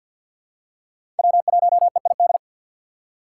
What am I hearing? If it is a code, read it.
Morse code, heard as a keyed tone, about 50 wpm, 700 Hz. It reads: W1EID
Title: USA Call Signs - 6x Character Spacing - 50wpm